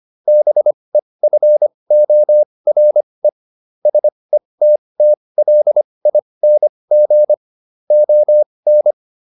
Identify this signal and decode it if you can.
Morse code, heard as a keyed tone, about 25 wpm, spaced as for 20 wpm, 600 Hz. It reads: BEFORE SETTLING ON